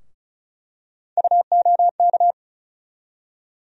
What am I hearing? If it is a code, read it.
Morse code, heard as a keyed tone, about 36 wpm, 700 Hz. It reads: UOK
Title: USA Call Sign Postfixes - 3 Letters 35wpm